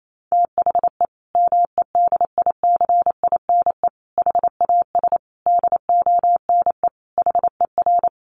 Morse code, heard as a keyed tone, about 28 wpm, 700 Hz. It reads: THE MEDICINE HAS DONE HER
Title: Sentences from Top 900 Words 28wpm